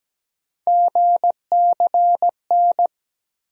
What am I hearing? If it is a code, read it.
Morse code, heard as a keyed tone, about 17 wpm, 700 Hz. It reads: GCN
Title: USA Call Sign Postfixes - 3 Letters 17wpm